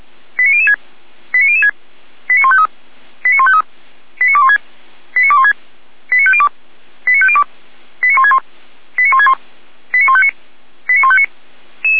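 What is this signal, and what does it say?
Five-tone selective call (ZVEI), heard as quick runs of five tones, tones of about 70 ms. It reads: 89006 89006 89153 89153 89216 89216 89502 89502 89161 89161 89159 89159
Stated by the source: ZVEI five-tone selective call received with an SDR.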